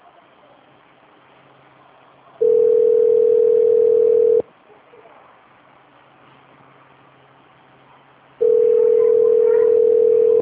Ringback tone (North America), 440 + 480 Hz.
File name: out- - - - - .2119